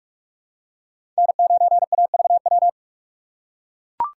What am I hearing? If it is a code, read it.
Morse code, heard as a keyed tone, about 44 wpm, 700 Hz. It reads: N9AVW
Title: USA Call Signs - 6x Character Spacing - 45wpm